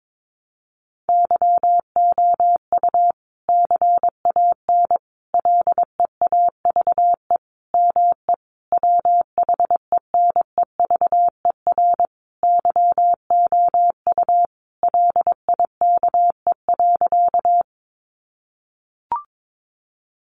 Morse code, heard as a keyed tone, about 22 wpm, 700 Hz. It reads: YOU CAN LEAVE ME WHENEVER YOU LIKE.
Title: Sentences from Top 1200 Words 22wpm